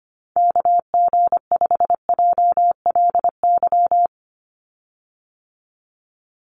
Morse code, heard as a keyed tone, about 25 wpm, 700 Hz. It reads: KG5JLY
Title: USA Call Signs 25wpm